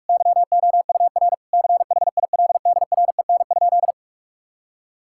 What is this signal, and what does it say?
Morse code, heard as a keyed tone, about 41 wpm, 700 Hz. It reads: YOUR CHILDREN?
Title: Sentences from Top 1000 Words - Review Part 6 - 45wpm